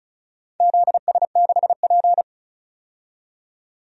Morse code, heard as a keyed tone, about 34 wpm, 700 Hz. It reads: ZS6P
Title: DXCC-100 Call Signs - 4 Characters - 35wpm